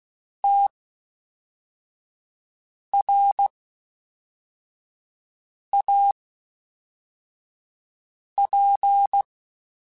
Morse code, heard as a keyed tone, about 16 wpm, spaced as for 4 wpm, 800 Hz. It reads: TRAP